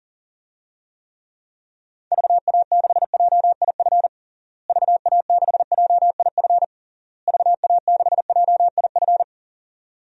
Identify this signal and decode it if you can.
Morse code, heard as a keyed tone, about 40 wpm, 700 Hz. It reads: VA6JIF VA6JIF VA6JIF
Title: DXCC-100 Call Signs - 6 Characters - Repeated 3x - 40wpm